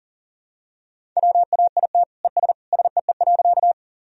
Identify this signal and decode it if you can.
Morse code, heard as a keyed tone, about 37 wpm, 700 Hz. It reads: WAIT ES SEE.
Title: Sentences from Top 500 Words and Abbreviations 40wpm